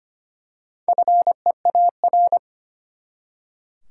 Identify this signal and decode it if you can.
Morse code, heard as a keyed tone, about 25 wpm, 700 Hz. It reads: FEAR
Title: Four Letter Words - 25wpm